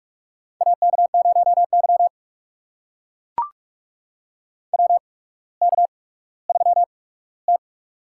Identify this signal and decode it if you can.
Morse code, heard as a keyed tone, about 44 wpm, spaced as for 19 wpm, 700 Hz. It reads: AK0Y WK3T
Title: USA Call Signs - 4 Characters - 8x Character Spacing - 45wpm